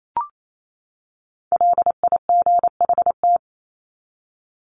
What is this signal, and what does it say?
Morse code, heard as a keyed tone, about 28 wpm, 700 Hz. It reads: LIGHT
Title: Top 300 Words 28wpm